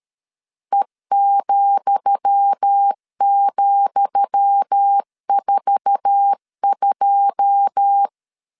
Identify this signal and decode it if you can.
Morse code, heard as a keyed tone, about 13 wpm, 800 Hz. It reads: E,,42
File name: LEZIONI / 10 SIGNS / 10D COMA 12wpm 2544